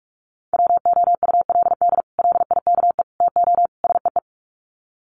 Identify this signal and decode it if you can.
Morse code, heard as a keyed tone, about 44 wpm, 700 Hz. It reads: WOULD LIKE TO SEE
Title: Sets of 4 Words 45wpm